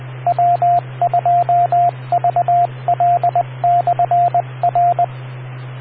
Morse code, heard as a keyed tone, about 20 wpm, 700 Hz, with radio noise. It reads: W2VL/R